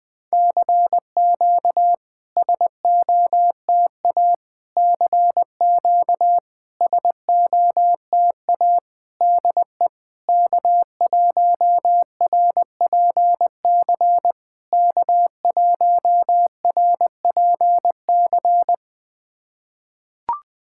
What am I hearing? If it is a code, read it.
Morse code, heard as a keyed tone, about 20 wpm, 690 Hz. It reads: CQ SOTA CQ SOTA DE K1RPC K1RPC
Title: SOTA QSO Practice 20wpm